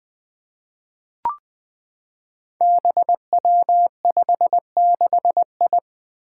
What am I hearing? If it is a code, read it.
Morse code, heard as a keyed tone, about 20 wpm, 700 Hz. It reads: BW56I
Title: Five Challenging Letter-Number Combinations 20wpm